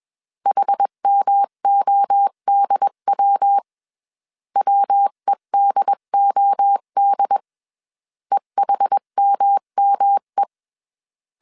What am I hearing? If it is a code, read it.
Morse code, heard as a keyed tone, about 21 wpm, 800 Hz. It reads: HMODW WEDOD EHMME
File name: 20wpm